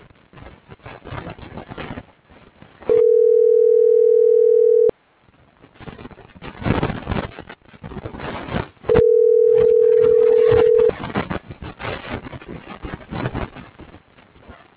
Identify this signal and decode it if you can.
Ringback tone (North America), 440 + 480 Hz.